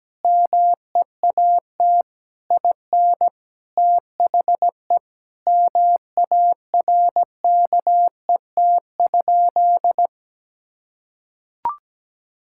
Morse code, heard as a keyed tone, about 17 wpm, 700 Hz. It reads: MEAT IN THE MARKET?
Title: Sentences from Top 1000 Words - Review Part 6 - 17wpm